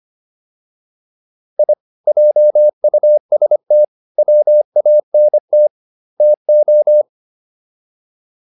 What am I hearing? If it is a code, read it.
Morse code, heard as a keyed tone, about 25 wpm, 600 Hz. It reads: I JUST WANT TO